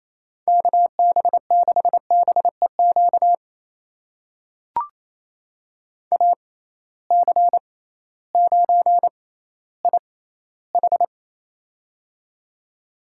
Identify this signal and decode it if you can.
Morse code, heard as a keyed tone, about 28 wpm, spaced as for 15 wpm, 700 Hz. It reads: KB6BEQ AC9IH
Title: USA Call Signs - 6x Character Spacing - 28wpm